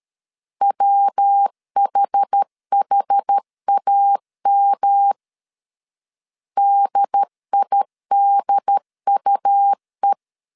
Morse code, heard as a keyed tone, about 13 wpm, 800 Hz. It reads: WHHAM DIDUE
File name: CW / DET / 1 3 1500C S C 12wpm 2750